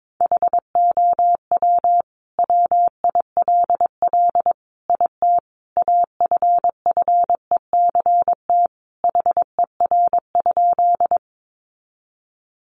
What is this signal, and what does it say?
Morse code, heard as a keyed tone, about 22 wpm, 700 Hz. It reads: HOW WILL IT AFFECT HER?
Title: Sentences from Top 900 Words 22wpm